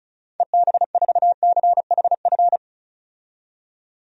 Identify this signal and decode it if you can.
Morse code, heard as a keyed tone, about 34 wpm, 700 Hz. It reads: EB4CHF
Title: DXCC-100 Call Signs - 6 Characters - 35wpm